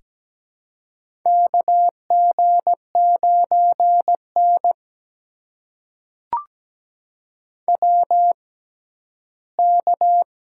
Morse code, heard as a keyed tone, about 17 wpm, spaced as for 12 wpm, 700 Hz. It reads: KG9N WK
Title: USA Call Signs - 4 Characters - 6x Character Spacing - 17wpm